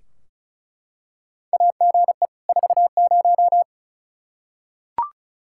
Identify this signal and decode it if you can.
Morse code, heard as a keyed tone, about 36 wpm, 700 Hz. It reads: AGE 40
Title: Practice QSO - Exchange 3 - More Info - 35wpm